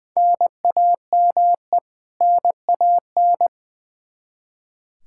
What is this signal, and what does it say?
Morse code, heard as a keyed tone, about 20 wpm, 700 Hz. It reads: NAME NAN
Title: Practice QSO - Exchange 1 - Beginning - 20wpm